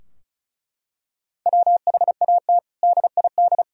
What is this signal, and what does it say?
Morse code, heard as a keyed tone, about 34 wpm, 700 Hz. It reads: WHAT DID
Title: Sentences from Top 1000 Words - Review Part 4 - 35wpm